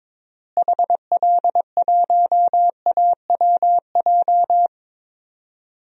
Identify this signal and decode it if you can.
Morse code, heard as a keyed tone, about 22 wpm, 700 Hz. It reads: HL1AWJ
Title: DXCC-100 Call Signs - 6 Characters - Repeated 2x - 22wpm